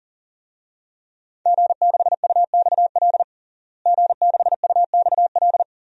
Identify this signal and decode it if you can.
Morse code, heard as a keyed tone, about 40 wpm, 700 Hz. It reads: G6UXL G6UXL
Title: DXCC-100 Call Signs - 5 Characters - Repeated 3x - 40wpm